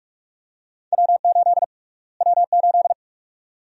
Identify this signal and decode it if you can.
Morse code, heard as a keyed tone, about 46 wpm, 700 Hz. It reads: W8 W8
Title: USA Call Sign Prefixes - Repeated 2x - 45wpm